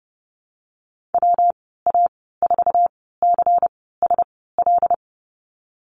Morse code, heard as a keyed tone, about 30 wpm, spaced as for 17 wpm, 700 Hz. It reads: WA4CSL
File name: Lesson 19g - Letter w - Call Signs 30wpm